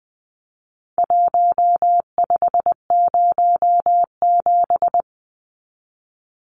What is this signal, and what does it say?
Morse code, heard as a keyed tone, about 20 wpm, 700 Hz. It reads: 1507
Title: Numbers - Contest Style - 4-Digit 20wpm